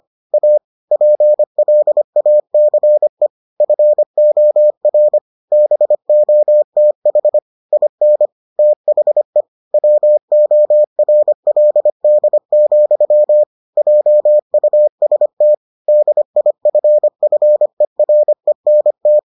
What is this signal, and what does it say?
Morse code, heard as a keyed tone, about 25 wpm, 600 Hz. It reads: A PLACE FOR BOTH IN THE WORLD, JUST DIFFERENT